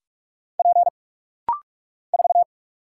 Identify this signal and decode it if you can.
Morse code, heard as a keyed tone, about 46 wpm, 700 Hz. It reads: P 4
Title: Single Letter-Number Alt - Rapid-Fire 45wpm